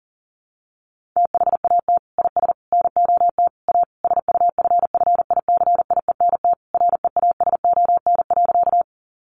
Morse code, heard as a keyed tone, about 37 wpm, 700 Hz. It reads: THAT IS NOT A SUFFICIENT REASON.
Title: Sentences from Top 1400 Words 40wpm